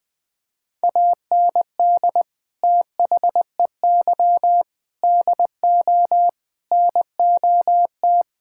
Morse code, heard as a keyed tone, about 20 wpm, 700 Hz. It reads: AND THEY DO NOT
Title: Sentences with 4-Word Phrases - 20wpm